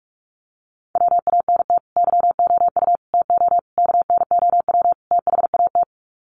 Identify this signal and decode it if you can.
Morse code, heard as a keyed tone, about 44 wpm, 700 Hz. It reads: WANT YOU TO KNOW THAT
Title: Sets of 5 Words Encore 45wpm